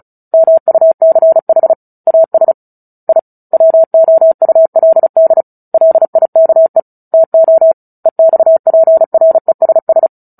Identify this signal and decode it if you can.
Morse code, heard as a keyed tone, about 36 wpm, 650 Hz. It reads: MUCH AS I WOULD LIKE TO EXPRESS
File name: Book Two / 09-Chapter 9 - 35 WPM